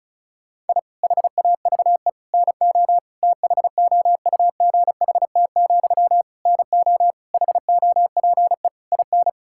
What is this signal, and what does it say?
Morse code, heard as a keyed tone, about 34 wpm, 700 Hz. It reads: I HAVE NO THOUGHT, NO HOPE IN
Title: Sentences from Top 1000 Words - Review Part 6 - 35wpm